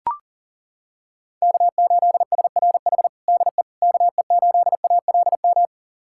Morse code, heard as a keyed tone, about 40 wpm, 700 Hz. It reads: K8SRH DE KE8ALM
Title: USA Callsign DE Callsign 40wpm